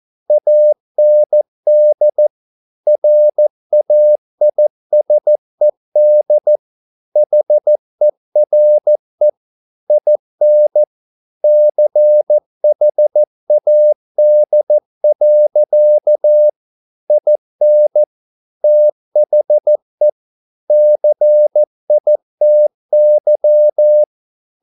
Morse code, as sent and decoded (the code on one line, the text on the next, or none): .- -. -.. / .-. .- .. ... -.. / .... .-. / .. -. / -.-. .... .- -.. .-.-.- / .. -. / - .... / -.-. .. - -.--
AND RAISED HERE IN CHAD. IN THE CITY